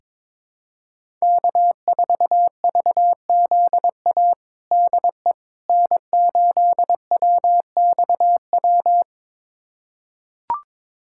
Morse code, heard as a keyed tone, about 22 wpm, 700 Hz. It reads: K4VZA DE N8WXW
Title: USA Callsign DE Callsign - 5 Characters - 22wpm